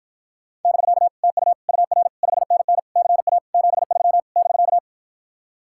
Morse code, heard as a keyed tone, about 54 wpm, 700 Hz. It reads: <BK> TU UR 5NN CA 73 <BK>